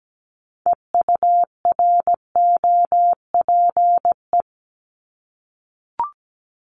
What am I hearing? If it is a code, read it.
Morse code, heard as a keyed tone, about 17 wpm, 700 Hz. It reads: EUROPE
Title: Multi-letter Phonemes 17wpm